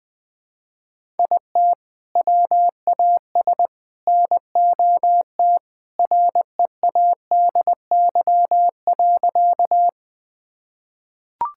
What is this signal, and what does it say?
Morse code, heard as a keyed tone, about 21 wpm, 700 Hz. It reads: IT WAS NOT READY.
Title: Sentences with 2-Word Phrases - Spoken Beginning - 20wpm